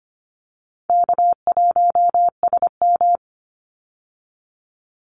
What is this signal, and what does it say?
Morse code, heard as a keyed tone, about 25 wpm, 700 Hz. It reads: K1SM
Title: Easy Call Signs Repeated 2x - 25wpm